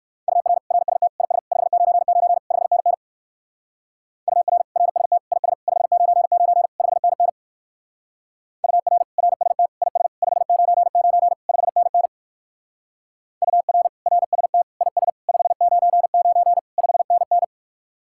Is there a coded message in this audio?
Yes: Morse code from 0.3 to 17.4 s, heard as a keyed tone, about 57 wpm, 700 Hz. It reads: UR RST IS 599 5NN UR RST IS 599 5NN UR RST IS 599 5NN UR RST IS 599 5NN